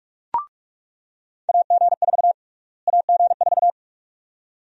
Morse code, heard as a keyed tone, about 46 wpm, 700 Hz. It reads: AG4 AG4
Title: USA Call Sign Prefixes - Repeated 2x - 45wpm